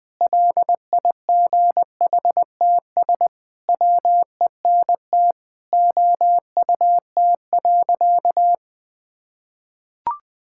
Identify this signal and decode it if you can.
Morse code, heard as a keyed tone, about 20 wpm, 700 Hz. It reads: LIGHTS WENT OUT.
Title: Sentences with 6-Word Phrases - 20wpm